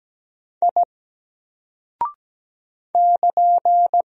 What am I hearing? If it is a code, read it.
Morse code, heard as a keyed tone, about 17 wpm, 700 Hz. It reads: I <KN>
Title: Single Character (Letter:Number:Punctuation:Prosign) Alt 17wpm